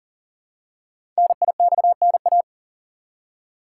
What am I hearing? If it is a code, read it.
Morse code, heard as a keyed tone, about 43 wpm, 700 Hz. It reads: NIXNA
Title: Code Groups - 5 Letters - 40wpm